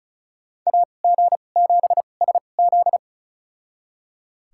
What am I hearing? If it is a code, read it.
Morse code, heard as a keyed tone, about 36 wpm, spaced as for 25 wpm, 700 Hz. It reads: AG7SZ
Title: USA Call Signs - 5 Characters - 2x Character Spacing - 35wpm